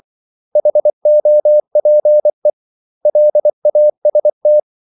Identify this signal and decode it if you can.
Morse code, heard as a keyed tone, about 24 wpm, 600 Hz. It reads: HOPE LAST
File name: CW / Practice / 500MostCommonEnglishWords24